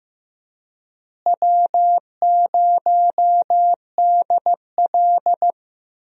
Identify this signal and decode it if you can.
Morse code, heard as a keyed tone, about 15 wpm, 700 Hz. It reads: W0DL
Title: USA Call Signs - 4 Characters - Repeated 2x - 15wpm